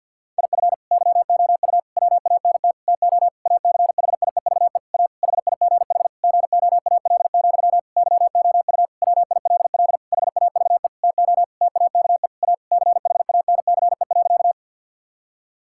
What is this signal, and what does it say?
Morse code, heard as a keyed tone, about 51 wpm, 700 Hz. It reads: IF YOU WANT TO ACHIEVE A HIGH GOAL, YOU WILL HAVE TO TAKE A CHANCE.
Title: Sentences from Top 1000 Words - Review Part 6 - 50wpm